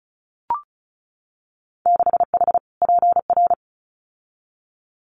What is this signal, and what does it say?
Morse code, heard as a keyed tone, about 36 wpm, 700 Hz. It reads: 6H PR
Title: Winter Field Day Exchanges Encore 35wpm